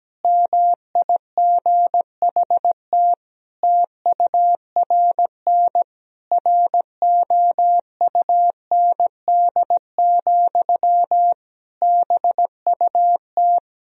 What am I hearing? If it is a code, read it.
Morse code, heard as a keyed tone, about 17 wpm, 700 Hz. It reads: MIGHT TURN ROUND, BUT
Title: Sentences from Top 1000 Words - Review Part 1 - 17wpm